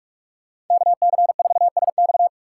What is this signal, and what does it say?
Morse code, heard as a keyed tone, about 42 wpm, 700 Hz. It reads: KC4SX